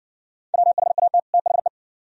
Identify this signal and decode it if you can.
Morse code, heard as a keyed tone, about 60 wpm, 690 Hz. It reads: WHAT THE